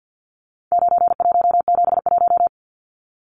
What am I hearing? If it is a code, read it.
Morse code, heard as a keyed tone, about 50 wpm, 700 Hz. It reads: 9171
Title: Numbers - 4-Digit 50wpm